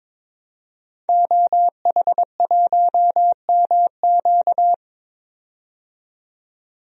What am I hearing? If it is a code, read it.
Morse code, heard as a keyed tone, about 21 wpm, 700 Hz. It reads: OH1MQ
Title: DXCC-100 Call Signs - 5 Characters - 22wpm